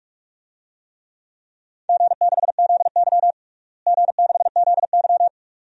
Morse code, heard as a keyed tone, about 47 wpm, 700 Hz. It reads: G6ZY G6ZY
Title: DXCC-100 Call Signs - 4 Characters - Repeated 2x - 45wpm